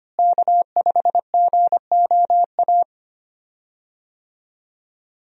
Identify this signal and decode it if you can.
Morse code, heard as a keyed tone, about 25 wpm, 700 Hz. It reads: K5GOA
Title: USA Call Signs - 5 Characters - Repeated 3x - 25wpm